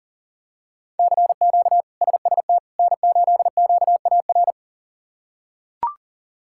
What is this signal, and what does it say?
Morse code, heard as a keyed tone, about 40 wpm, 700 Hz. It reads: CQ SST N8QAR
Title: K1USN SST QSO Practice - 40wpm